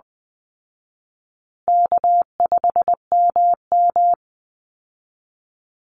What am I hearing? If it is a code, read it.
Morse code, heard as a keyed tone, about 20 wpm, 700 Hz. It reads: K5MM
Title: Easy Call Signs Repeated 3x - 20wpm